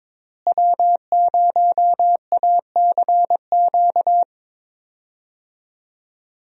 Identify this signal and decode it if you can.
Morse code, heard as a keyed tone, about 22 wpm, 700 Hz. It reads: W0ACQ
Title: USA Call Signs - 5 Characters - Repeated 3x - 22wpm